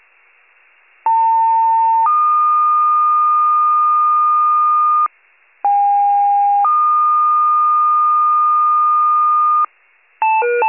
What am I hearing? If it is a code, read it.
Two-tone page, heard as a one-second tone, then a three-second tone, about 900 then 1220 Hz, about 810 then 1220 Hz. It is heard from 1.0 to 9.6 s.
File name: wcalert